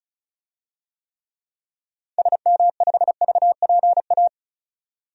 Morse code, heard as a keyed tone, about 36 wpm, 700 Hz. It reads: SM5VPA
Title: DXCC-100 Call Signs - 6 Characters - 35wpm